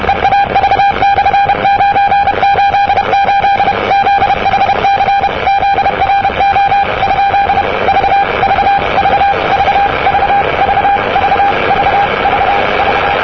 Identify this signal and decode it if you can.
Morse code, heard as a keyed tone, about 31 wpm, 800 Hz, with radio noise. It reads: EAV/988 ZHC ZRO ? V V V V V V V V V V
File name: europe / e-539-cw-eav988-rb-1973